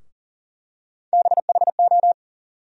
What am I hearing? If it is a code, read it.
Morse code, heard as a keyed tone, about 40 wpm, 700 Hz. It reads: BHO
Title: USA Call Sign Postfixes - 3 Letters 40wpm